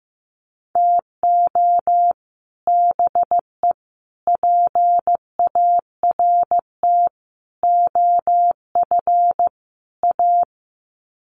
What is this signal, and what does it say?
Morse code, heard as a keyed tone, about 15 wpm, 700 Hz. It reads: TO BE PART OF A
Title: Sets of 5 Words Encore2 15wpm